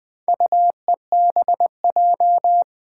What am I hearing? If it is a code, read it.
Morse code, heard as a keyed tone, about 20 wpm, 700 Hz. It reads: UEBJ
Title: Four Challenging Letter-Number Combinations 20wpm